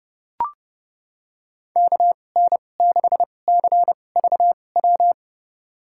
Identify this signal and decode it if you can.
Morse code, heard as a keyed tone, about 30 wpm, spaced as for 22 wpm, 700 Hz. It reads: KN6CVW
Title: USA Call Signs - 6 Characters - 2x Character Spacing - 30wpm